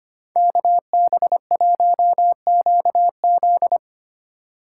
Morse code, heard as a keyed tone, about 29 wpm, 700 Hz. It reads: KB1QZ
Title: Speed Racing - USA Call Signs 25wpm